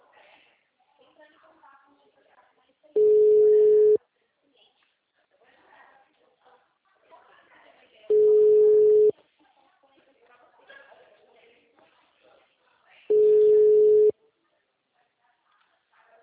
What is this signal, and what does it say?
Ringback tone (Europe, 425 Hz), with speech.